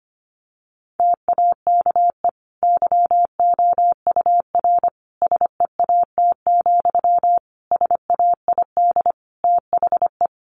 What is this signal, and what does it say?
Morse code, heard as a keyed tone, about 25 wpm, 700 Hz. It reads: TAKE YOUR SEAT, SAID THE
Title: Sentences from Top 1500 Words 25wpm